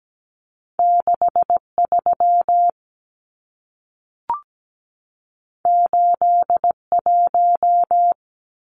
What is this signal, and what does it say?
Morse code, heard as a keyed tone, about 17 wpm, 700 Hz. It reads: 63 81
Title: Numbers - 2-Digit 17wpm